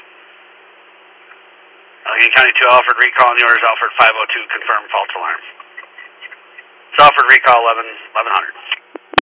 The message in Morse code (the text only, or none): none